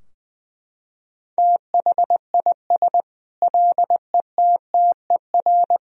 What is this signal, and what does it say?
Morse code, heard as a keyed tone, about 20 wpm, 700 Hz. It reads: THIS LETTER
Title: Sentences from Top 1000 Words - Review Part 4 - 20wpm